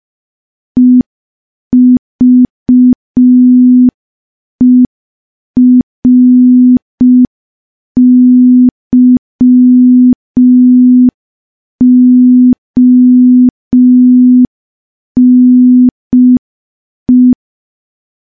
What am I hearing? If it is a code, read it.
Morse code, heard as a keyed tone, about 5 wpm, 250 Hz. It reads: EVERYONE
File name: morse / morse-training-05WPM